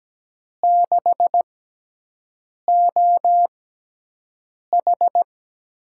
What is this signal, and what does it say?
Morse code, heard as a keyed tone, about 17 wpm, spaced as for 6 wpm, 700 Hz. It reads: 6OH
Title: USA Call Signs - 4 Characters - 6x Character Spacing - 17wpm